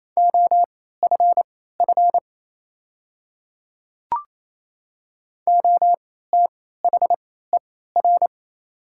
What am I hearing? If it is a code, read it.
Morse code, heard as a keyed tone, about 28 wpm, spaced as for 16 wpm, 700 Hz. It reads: OFF OTHER
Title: 3x Character Spacing - Top 200 Words - 28wpm